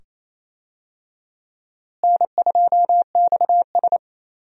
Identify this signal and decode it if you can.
Morse code, heard as a keyed tone, about 28 wpm, 700 Hz. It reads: N2XS
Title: USA Call Signs - 4 Characters - Repeated 2x - 28wpm